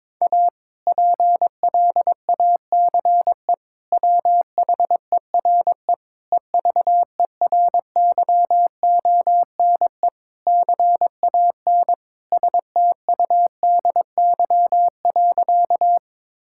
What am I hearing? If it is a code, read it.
Morse code, heard as a keyed tone, about 21 wpm, 700 Hz. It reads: A PLACE WHERE EVERYONE CAN STUDY.
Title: Sentences from Top 1000 Words - Review Part 5 - 22wpm